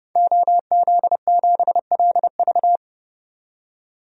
Morse code, heard as a keyed tone, about 30 wpm, 700 Hz. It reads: OZ7LV
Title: DXCC-100 Call Signs - 5 Characters - 30wpm